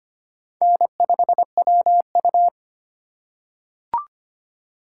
Morse code, heard as a keyed tone, about 25 wpm, 700 Hz. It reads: N5WU
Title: USA Call Signs - 4 Characters - 3x Character Spacing - 25wpm